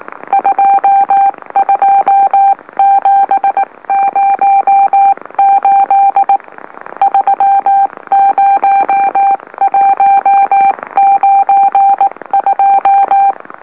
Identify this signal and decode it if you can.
Morse code, heard as a keyed tone, about 18 wpm, 790 Hz. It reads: 22708 30192
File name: M04- -2345utc-LOLO